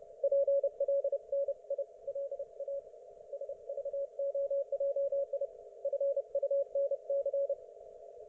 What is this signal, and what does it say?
Morse code, heard as a keyed tone, about 29 wpm, 560 Hz. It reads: PLNILA SVOJI FUNC